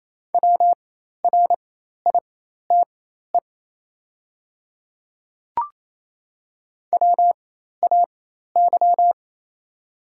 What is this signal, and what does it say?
Morse code, heard as a keyed tone, about 28 wpm, spaced as for 13 wpm, 700 Hz. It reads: WRITE WAY
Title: 4x Character Spacing - Top 100 Words - 28wpm